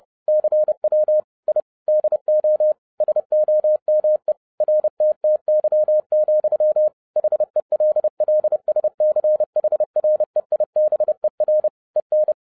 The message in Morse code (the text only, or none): CW I DO SOME RTTY, HELLSCHREIBER EN